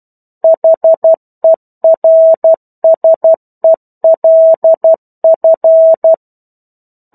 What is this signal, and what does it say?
Morse code, heard as a keyed tone, about 12 wpm, 650 Hz. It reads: HERSELF
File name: 01-Chapter-01 12 WPM